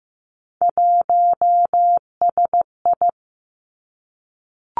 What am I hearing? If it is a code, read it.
Morse code, heard as a keyed tone, about 15 wpm, 700 Hz. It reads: 1SI
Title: Three Challenging Letter-Number Combinations 15wpm